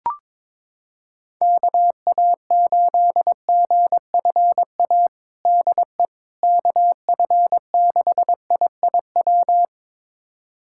Morse code, heard as a keyed tone, about 21 wpm, 700 Hz. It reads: KA8GFA DE KF6IIW
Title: USA Callsign DE Callsign 22wpm